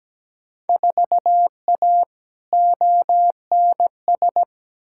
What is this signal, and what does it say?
Morse code, heard as a keyed tone, about 17 wpm, 700 Hz. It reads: 4A ONS
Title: Field Day Exchanges - 17wpm